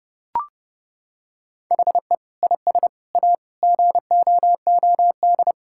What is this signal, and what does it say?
Morse code, heard as a keyed tone, about 30 wpm, 700 Hz. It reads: HE IS A GOOD
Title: Sentences from Top 1600 Words 30wpm